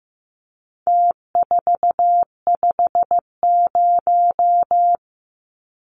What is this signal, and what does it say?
Morse code, heard as a keyed tone, about 15 wpm, 700 Hz. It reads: T450
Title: Numbers - Contest Style - 4-Digit 15wpm